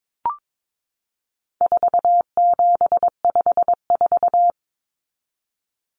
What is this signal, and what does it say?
Morse code, heard as a keyed tone, about 22 wpm, 700 Hz. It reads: 4754
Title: Common Cut Numbers - 4-Digit 22wpm